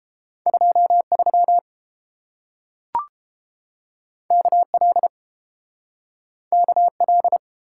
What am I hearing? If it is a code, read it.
Morse code, heard as a keyed tone, about 31 wpm, 700 Hz. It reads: 23 KL KL